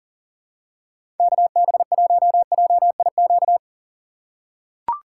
Morse code, heard as a keyed tone, about 39 wpm, 700 Hz. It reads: KB1JIQ
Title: USA Call Signs - 6 Characters - 2x Character Spacing - 40wpm